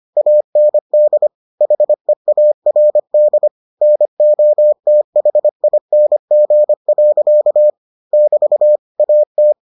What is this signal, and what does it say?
Morse code, heard as a keyed tone, about 25 wpm, 600 Hz. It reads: AND HEARD NOTHING. <BT> AT